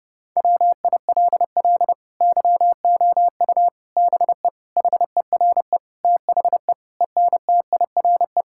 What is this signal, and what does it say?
Morse code, heard as a keyed tone, about 30 wpm, 700 Hz. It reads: WILL YOU BE HERE THE ENTIRE
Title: Sentences from Top 1000 Words - Review Part 5 - 30wpm